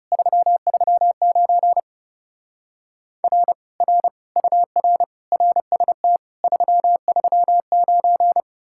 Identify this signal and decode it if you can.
Morse code, heard as a keyed tone, about 37 wpm, 700 Hz. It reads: 339 R R UR RST 339
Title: Speed-Racing Common QSO Phrases - 30wpm